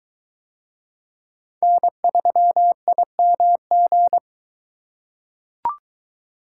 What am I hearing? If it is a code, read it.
Morse code, heard as a keyed tone, about 21 wpm, 700 Hz. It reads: N3IMG